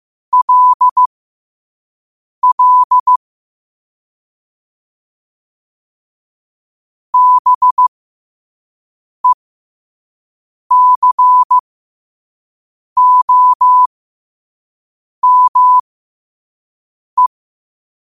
Morse code, heard as a keyed tone, about 15 wpm, spaced as for 5 wpm, 1000 Hz. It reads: LL BECOME
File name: qotdfiles / 5wpm